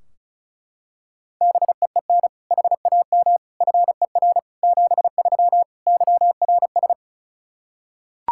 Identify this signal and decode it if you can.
Morse code, heard as a keyed tone, about 34 wpm, 700 Hz. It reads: BEEN HAM FER 73 YRS
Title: Practice QSO - Exchange 3 - More Info - 35wpm